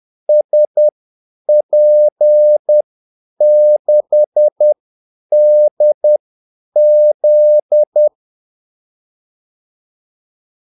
Morse code, heard as a keyed tone, about 10 wpm, spaced as for 8 wpm, 600 Hz. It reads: SP6DZ
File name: Callsigns-10-8wpm